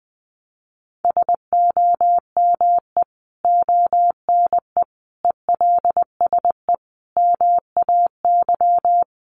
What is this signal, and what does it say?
Morse code, heard as a keyed tone, about 20 wpm, 700 Hz. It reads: SOME ONE ELSE MAY
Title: Sentences from Top 1100 Words 20wpm